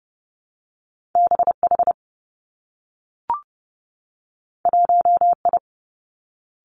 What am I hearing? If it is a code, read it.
Morse code, heard as a keyed tone, about 30 wpm, 700 Hz. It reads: B H 1 I
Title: Two Challenging Letter-Number Combinations 30wpm